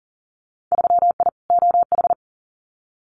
Morse code, heard as a keyed tone, about 40 wpm, 700 Hz. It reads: 3I OH
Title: Winter Field Day Exchanges Encore 40wpm